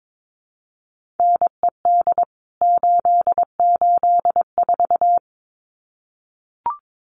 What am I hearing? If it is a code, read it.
Morse code, heard as a keyed tone, about 22 wpm, 700 Hz. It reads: NED 884
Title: CWT Exchange 22wpm